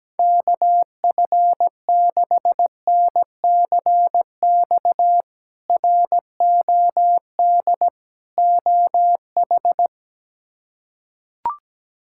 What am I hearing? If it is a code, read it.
Morse code, heard as a keyed tone, about 17 wpm, 700 Hz. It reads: KF6NCX ROD OH
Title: K1USN SST QSO Practice - 17wpm